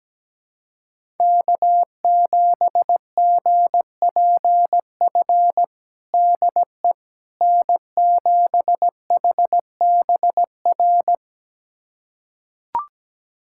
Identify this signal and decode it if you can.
Morse code, heard as a keyed tone, about 17 wpm, 700 Hz. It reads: K7GPF DE N7HBR
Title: USA Callsign DE Callsign - 5 Characters - 17wpm